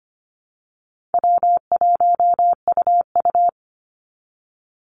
Morse code, heard as a keyed tone, about 25 wpm, 700 Hz. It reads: W1UU
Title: Call Signs Repeated 2x - 25wpm